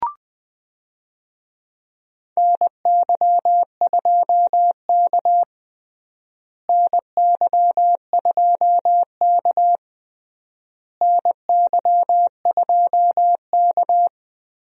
Morse code, heard as a keyed tone, about 20 wpm, 700 Hz. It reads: NY2K NY2K NY2K
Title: USA Call Signs - 4 Characters - Repeated 3x - 20wpm